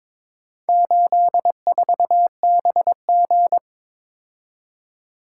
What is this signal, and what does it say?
Morse code, heard as a keyed tone, about 22 wpm, 700 Hz. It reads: 84BG
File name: practice / 22wpm / Four Letter-Number Combinations - 22wpm